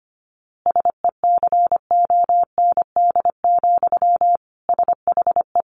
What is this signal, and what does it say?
Morse code, heard as a keyed tone, about 25 wpm, 700 Hz. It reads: SECOND, SHE
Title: Easy Phrases 25wpm